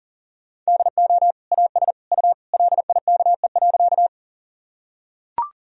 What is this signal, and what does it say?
Morse code, heard as a keyed tone, about 40 wpm, 700 Hz. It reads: DO AS U LIKE.
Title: Sentences from Top 100 Words and Abbreviations 40wpm